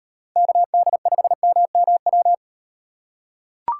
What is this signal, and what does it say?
Morse code, heard as a keyed tone, about 35 wpm, 700 Hz. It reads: KD5MMW